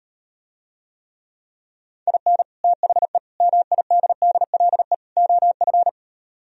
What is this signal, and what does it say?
Morse code, heard as a keyed tone, about 39 wpm, 700 Hz. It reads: IN THE MIDDLE OF